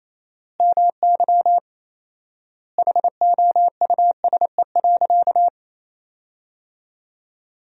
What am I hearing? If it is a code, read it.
Morse code, heard as a keyed tone, about 28 wpm, 700 Hz. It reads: MY HOUSE.
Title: 4x Word-Spacing - Sentences from Top 200 Words - 28wpm